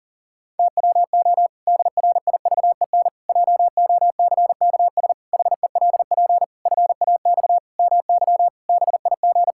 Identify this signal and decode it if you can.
Morse code, heard as a keyed tone, about 37 wpm, 700 Hz. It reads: TWO DRIVEN JOCKS HELP FAX MY BIG
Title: Pangram Challenge v2 - 40wpm